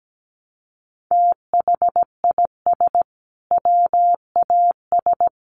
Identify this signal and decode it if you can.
Morse code, heard as a keyed tone, about 17 wpm, 700 Hz. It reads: THIS WAS
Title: Sentences from Top 700 Words 17wpm